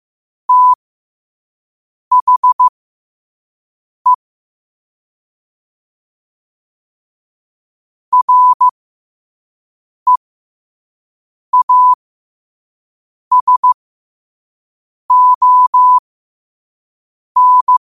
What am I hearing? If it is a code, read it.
Morse code, heard as a keyed tone, about 15 wpm, spaced as for 5 wpm, 1000 Hz. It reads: THE REASON
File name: qotdfiles / 5wpm